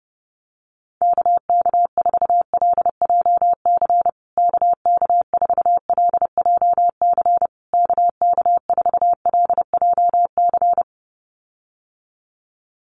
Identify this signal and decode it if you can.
Morse code, heard as a keyed tone, about 30 wpm, 700 Hz. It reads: KK4LJC KK4LJC KK4LJC
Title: USA Call Signs Repeated 3x 30wpm